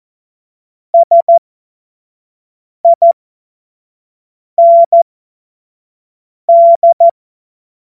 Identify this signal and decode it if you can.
Morse code, heard as a keyed tone, about 14 wpm, spaced as for 5 wpm, 680 Hz. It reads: SIND